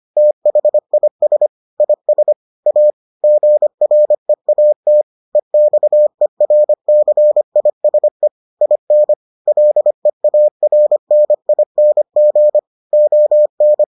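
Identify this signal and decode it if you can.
Morse code, heard as a keyed tone, about 25 wpm, 600 Hz. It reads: THIS IS A GREAT EXERCISE IN LEARNING ON
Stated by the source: Morse edition of the Foundations of Amateur Radio episodes.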